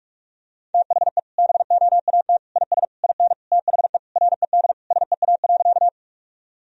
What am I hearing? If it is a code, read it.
Morse code, heard as a keyed tone, about 42 wpm, 700 Hz. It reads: THE BOAT IS IN THE RED SEA.
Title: Sentences from Top 1000 Words - Review Part 2 - 45wpm